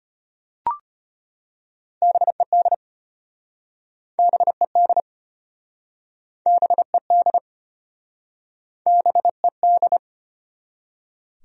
Morse code, heard as a keyed tone, about 33 wpm, 700 Hz. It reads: BED BED BED BED